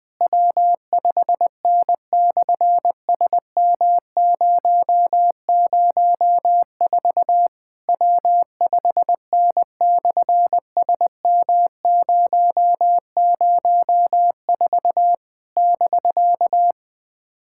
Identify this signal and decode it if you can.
Morse code, heard as a keyed tone, about 20 wpm, 700 Hz. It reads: W5N/SM004 W5N/SM004 <BK>
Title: SOTA QSO Practice 20wpm